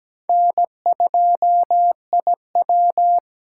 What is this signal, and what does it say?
Morse code, heard as a keyed tone, about 17 wpm, 700 Hz. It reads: N2IW
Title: USA Call Signs - 4 Characters - 2x Character Spacing - 17wpm